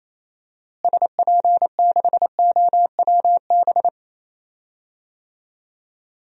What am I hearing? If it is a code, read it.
Morse code, heard as a keyed tone, about 28 wpm, 700 Hz. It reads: SP6OWB
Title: DXCC-100 Call Signs - 6 Characters - Repeated 2x - 28wpm